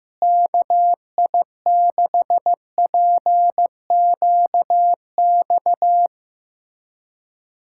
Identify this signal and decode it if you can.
Morse code, heard as a keyed tone, about 15 wpm, 700 Hz. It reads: KI6PQX
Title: USA Callsign DE Callsign - 6 Characters - 15wpm